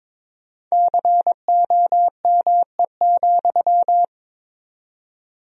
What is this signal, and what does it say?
Morse code, heard as a keyed tone, about 22 wpm, 700 Hz. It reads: COME,
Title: 4x Word-Spacing - Sentences from Top 500 Words - 22wpm